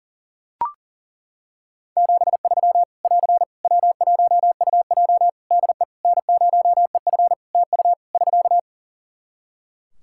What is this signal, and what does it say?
Morse code, heard as a keyed tone, about 41 wpm, 700 Hz. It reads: 73 <AR> W1UJ DE N0EF TU <SK>
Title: Practice QSO - Exchange 4 - Ending - 40wpm